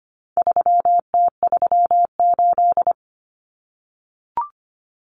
Morse code, heard as a keyed tone, about 25 wpm, 700 Hz. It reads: 3T38
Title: Common Cut Numbers - 4-Digit 25wpm